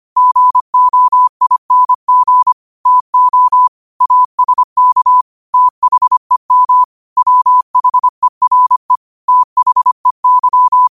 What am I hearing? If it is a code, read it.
Morse code, heard as a keyed tone, about 25 wpm, 1000 Hz. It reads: GOING TO ASK THEM WHERE THEY
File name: qotdfiles / 25wpm